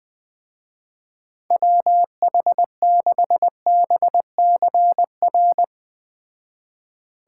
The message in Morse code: WH6BCR

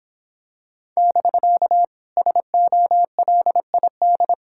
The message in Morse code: <BK> SOLID

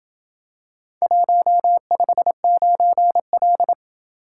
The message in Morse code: 159L

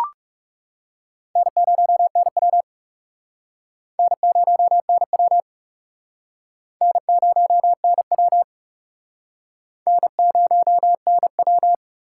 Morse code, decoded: N0NW N0NW N0NW N0NW

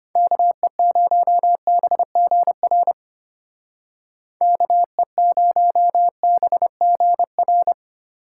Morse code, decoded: KE0BGR KE0BGR

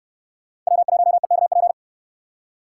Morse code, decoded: W1EKP

